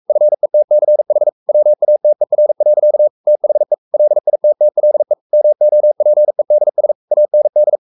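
Morse code, heard as a keyed tone, about 44 wpm, 600 Hz. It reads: FETCH WATER. THE LITTLE MOPEDS AND
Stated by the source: Morse code at 43 wpm sending Ken Follett's novel Never in English, from CW med Gnister.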